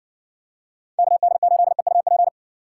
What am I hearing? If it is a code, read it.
Morse code, heard as a keyed tone, about 60 wpm, 700 Hz. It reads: KD8EUP